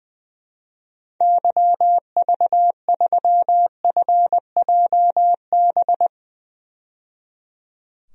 Morse code, heard as a keyed tone, about 21 wpm, 700 Hz. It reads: YV3FJB
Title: DXCC-100 Call Signs - 6 Characters - 20wpm